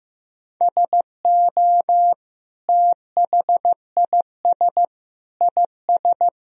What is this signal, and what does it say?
Morse code, heard as a keyed tone, about 15 wpm, 700 Hz. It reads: SO THIS IS
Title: Sentences from Top 1000 Words - Review Part 2 - 15wpm